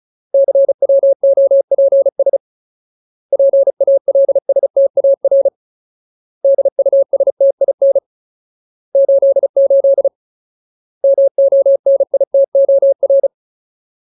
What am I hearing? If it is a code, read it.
Morse code, heard as a keyed tone, about 36 wpm, 550 Hz. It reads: CWOPS PALSTAR DUSTIN 88 MONITOR
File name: Ultimate Word List / 35WPM Ultimate